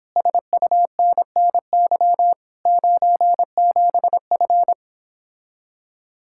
Morse code, heard as a keyed tone, about 29 wpm, 700 Hz. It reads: SUNNY 97F